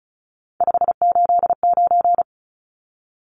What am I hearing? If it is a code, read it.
Morse code, heard as a keyed tone, about 34 wpm, 700 Hz. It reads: 589
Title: Numbers - 3-Digit 35wpm